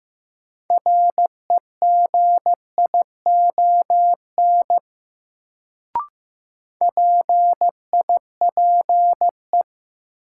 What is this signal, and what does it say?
Morse code, heard as a keyed tone, about 16 wpm, 700 Hz. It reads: REGION PIPE
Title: Sentences from Top 1500 Words - New Word Drill - 15wpm